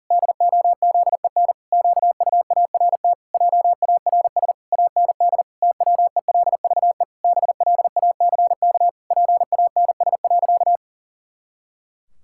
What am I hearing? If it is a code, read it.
Morse code, heard as a keyed tone, about 40 wpm, 700 Hz. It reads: DOZEN QUART JARS AND TWELVE BLACK PANS.
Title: Pangram Challenge v2 - 40wpm